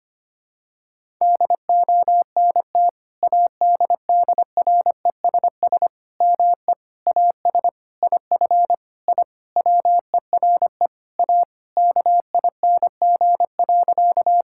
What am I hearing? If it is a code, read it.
Morse code, heard as a keyed tone, about 25 wpm, 700 Hz. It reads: DONT ADDRESS ME AS IF I WERE A KING.
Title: Sentences from Top 1000 Words - Review Part 6 - 25wpm